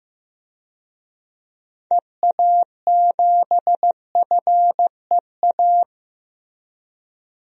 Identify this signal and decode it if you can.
Morse code, heard as a keyed tone, about 16 wpm, 700 Hz. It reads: EA7FEA
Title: DXCC-100 Call Signs - 6 Characters - Repeated 3x - 15wpm